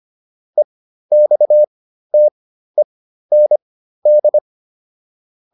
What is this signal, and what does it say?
Morse code, heard as a keyed tone, about 25 wpm, spaced as for 13 wpm, 600 Hz. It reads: EXTEND